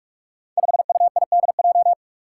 Morse code, heard as a keyed tone, about 44 wpm, 700 Hz. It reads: 5UIDJ
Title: Five Challenging Letter-Number Combinations 45wpm